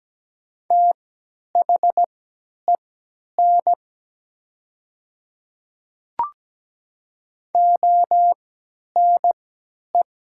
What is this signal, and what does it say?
Morse code, heard as a keyed tone, about 17 wpm, spaced as for 10 wpm, 700 Hz. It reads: THEN ONE
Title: 3x Character Spacing - Top 100 Words - 17wpm